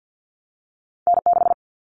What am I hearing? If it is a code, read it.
Morse code, heard as a keyed tone, about 50 wpm, 700 Hz. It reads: N6
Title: Callsign Prefixes for Top 25 DXCC - Repeated 3x - 50wpm